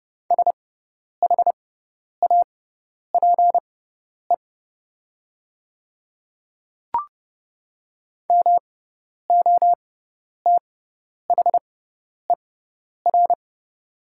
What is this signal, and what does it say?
Morse code, heard as a keyed tone, about 30 wpm, spaced as for 10 wpm, 700 Hz. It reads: SHAPE MOTHER
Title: 6x Character Spacing - Top 500 Words - 30wpm